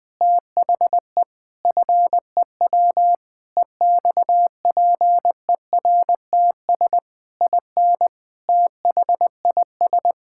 Morse code, heard as a keyed tone, about 20 wpm, 700 Hz. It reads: THE FEW EXPERTS IN THIS
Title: Sentences with 4-Word Phrases - 20wpm